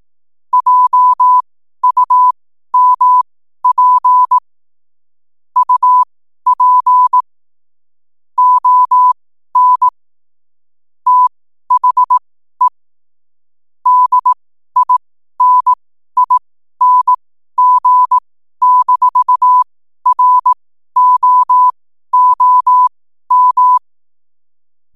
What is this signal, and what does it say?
Morse code, heard as a keyed tone, about 18 wpm, spaced as for 12 wpm, 1000 Hz. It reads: JUMP UP ON THE DINING-ROOM
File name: n7net / 12wpm